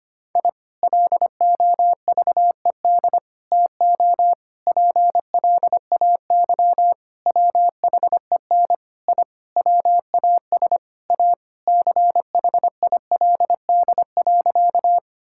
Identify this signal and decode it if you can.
Morse code, heard as a keyed tone, about 25 wpm, 700 Hz. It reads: I LOVED TO PLAY WHEN I WAS A CHILD.
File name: practice / 25wpm / Sentences with 4-Word Phrases - 25wpm